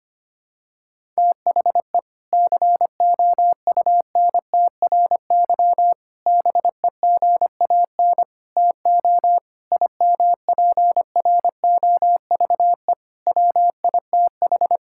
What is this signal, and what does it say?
Morse code, heard as a keyed tone, about 25 wpm, 700 Hz. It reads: THE COUNTRY BEGAN TO IMPROVE WITH
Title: Sentences from Top 1000 Words - Review Part 5 - 25wpm